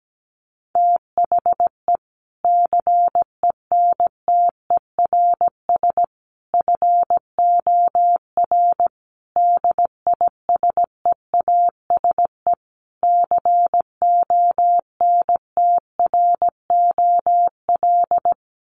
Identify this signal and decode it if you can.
Morse code, heard as a keyed tone, about 17 wpm, 700 Hz. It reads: THE CENTERS FOR DISEASE CONTROL
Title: Sets of 5 Words Encore2 17wpm